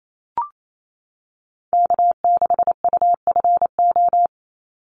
Morse code, heard as a keyed tone, about 28 wpm, 700 Hz. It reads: K6UFO
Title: Call Signs - 28wpm